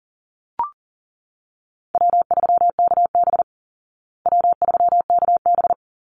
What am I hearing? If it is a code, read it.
Morse code, heard as a keyed tone, about 40 wpm, 700 Hz. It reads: W3KB W3KB
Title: Easy Call Signs Repeated 2x - 40wpm